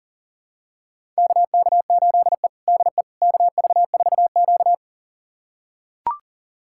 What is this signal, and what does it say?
Morse code, heard as a keyed tone, about 40 wpm, 700 Hz. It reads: KK8E DE KV4Q
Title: USA Callsign DE Callsign - 4 Characters - 40wpm